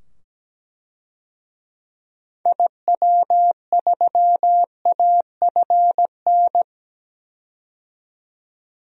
Morse code, heard as a keyed tone, about 17 wpm, 700 Hz. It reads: IW3AFN
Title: DXCC-100 Call Signs - 6 Characters - 17wpm